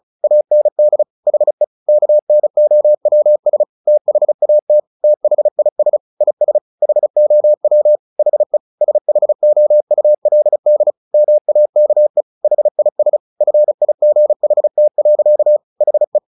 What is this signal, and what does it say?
Morse code, heard as a keyed tone, about 34 wpm, 600 Hz. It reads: AND HE KNOWS THAT THIS IS HOW HE SHOULD MAKE HIS FIGHT. HE
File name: the old man and the sea / the old man and the sea 1 35wpm